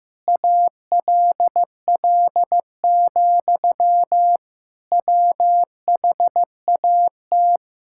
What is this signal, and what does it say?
Morse code, heard as a keyed tone, about 15 wpm, 700 Hz. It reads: ALL, WHAT
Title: Sentences from Top 1600 Words 15wpm